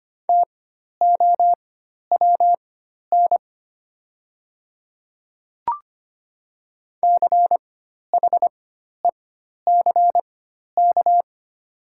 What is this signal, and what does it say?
Morse code, heard as a keyed tone, about 25 wpm, spaced as for 12 wpm, 700 Hz. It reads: TOWN CHECK
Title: 4x Character Spacing - Top 500 Words - 25wpm